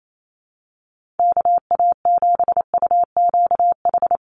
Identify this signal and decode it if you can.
Morse code, heard as a keyed tone, about 28 wpm, 700 Hz. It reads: KA7UQH
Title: USA Call Signs Repeated 2x 28wpm